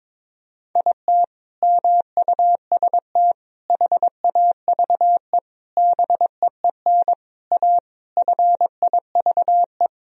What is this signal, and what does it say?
Morse code, heard as a keyed tone, about 22 wpm, 700 Hz. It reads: IT MUST HAVE BEEN A FIVE
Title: Sentences from Top 1000 Words - Review Part 2 - 22wpm